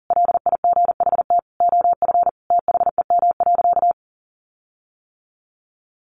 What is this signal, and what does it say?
Morse code, heard as a keyed tone, about 40 wpm, 700 Hz. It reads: LIGHT OF THEM.
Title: Sentences from Top 1300 Words 40wpm